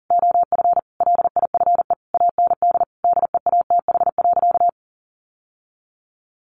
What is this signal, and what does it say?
Morse code, heard as a keyed tone, about 37 wpm, 700 Hz. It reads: OF LIFE AND DEATH.
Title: Sentences from Top 1100 Words 40wpm